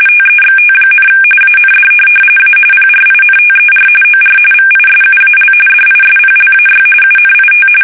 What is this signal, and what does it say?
RTTY, about 45.45 baud. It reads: TEXT, AND A PRINTER OR VISUAL DISPLAY UNIT
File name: fsk850 / teletype-RTTY